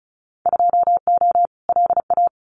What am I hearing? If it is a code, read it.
Morse code, heard as a keyed tone, about 34 wpm, 700 Hz. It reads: 2O LA
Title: Winter Field Day Exchanges 35wpm